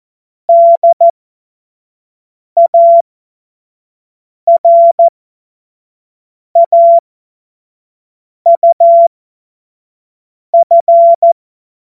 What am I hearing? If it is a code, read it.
Morse code, heard as a keyed tone, about 14 wpm, spaced as for 5 wpm, 680 Hz. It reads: DARAUF
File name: Morsekurs / Witze / Witz 01. August 2023 25 F70